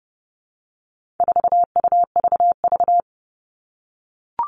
Morse code, heard as a keyed tone, about 30 wpm, 700 Hz. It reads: 4UVV
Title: All Cut Numbers - 4-Digit 30wpm